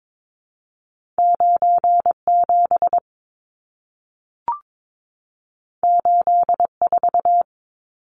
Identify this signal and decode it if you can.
Morse code, heard as a keyed tone, about 22 wpm, 700 Hz. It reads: 97 84
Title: Numbers - 2-Digit 22wpm